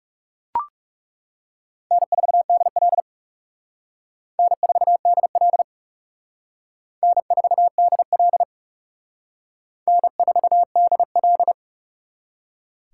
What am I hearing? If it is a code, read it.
Morse code, heard as a keyed tone, about 37 wpm, 700 Hz. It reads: N4DL N4DL N4DL N4DL
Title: Speed Racing - USA 1x2 Call Signs 30wpm